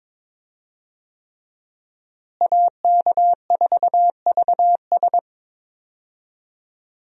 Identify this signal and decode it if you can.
Morse code, heard as a keyed tone, about 22 wpm, 700 Hz. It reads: AK4VS
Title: USA Call Signs - 5 Characters - Repeated 3x - 22wpm